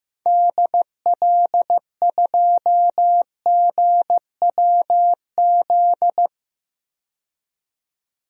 Morse code, heard as a keyed tone, about 15 wpm, 700 Hz. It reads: DL2GWZ
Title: DXCC-100 Call Signs - 6 Characters - 15wpm